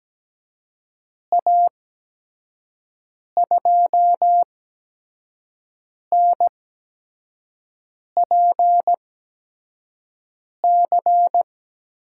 Morse code, heard as a keyed tone, about 17 wpm, spaced as for 5 wpm, 700 Hz. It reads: A2NPC
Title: USA Call Signs - 6 Characters - 8x Character Spacing - 17wpm